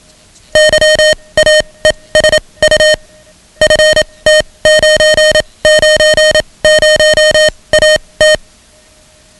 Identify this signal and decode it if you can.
Morse code, heard as a keyed tone, about 27 wpm, spaced as for 21 wpm, 620 Hz, with noise. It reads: YAESU FT990AT
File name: ham radio / CodeQuick / Code Quick 10